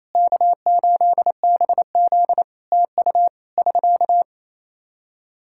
Morse code, heard as a keyed tone, about 28 wpm, 700 Hz. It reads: K8BZ TU <SK>
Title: Practice QSO - Exchange 4 - Ending - 28wpm